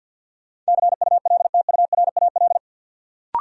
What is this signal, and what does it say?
Morse code, heard as a keyed tone, about 52 wpm, 700 Hz. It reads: CULTURAL
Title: Sentences from Top 1500 Words - New Word Drill - 50wpm